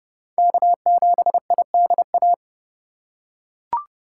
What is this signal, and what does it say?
Morse code, heard as a keyed tone, about 30 wpm, 700 Hz. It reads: K7IDA
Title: USA Call Signs - 3x Character Spacing - 30wpm